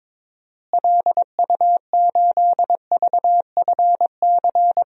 Morse code, heard as a keyed tone, about 22 wpm, 700 Hz. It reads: LU8VFC
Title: DXCC-100 Call Signs - 6 Characters - Repeated 3x - 22wpm